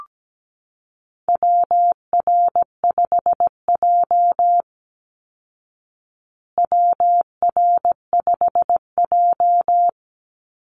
Morse code, heard as a keyed tone, about 17 wpm, 700 Hz. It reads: WR5J WR5J
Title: Call Signs Repeated 2x - 17wpm